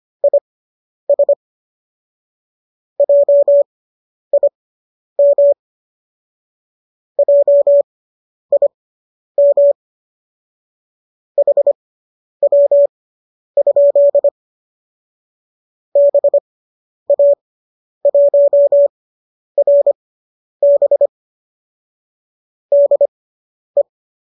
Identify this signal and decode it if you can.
Morse code, heard as a keyed tone, about 25 wpm, spaced as for 10 wpm, 570 Hz. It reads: IS JIM JIM HW? BA1RB DE